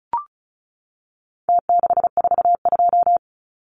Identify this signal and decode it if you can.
Morse code, heard as a keyed tone, about 34 wpm, 700 Hz. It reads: T642
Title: Numbers - Contest Style - 4-Digit 35wpm